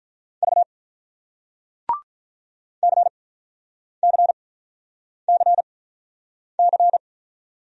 Morse code, heard as a keyed tone, about 46 wpm, spaced as for 9 wpm, 700 Hz. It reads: V CCCC